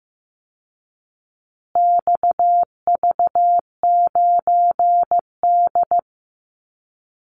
Morse code, heard as a keyed tone, about 15 wpm, 700 Hz. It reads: XV9D
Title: Call Signs Repeated 3x - 15wpm